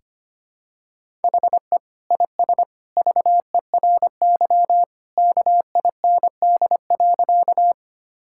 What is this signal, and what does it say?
Morse code, heard as a keyed tone, about 25 wpm, 700 Hz. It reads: HE IS VERY KIND.
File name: practice / 25wpm / Sentences with 2-Word Phrases - 25wpm